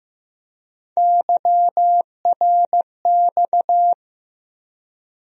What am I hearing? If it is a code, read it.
Morse code, heard as a keyed tone, about 15 wpm, 700 Hz. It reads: YRX
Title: USA Call Sign Postfixes - 3 Letters 15wpm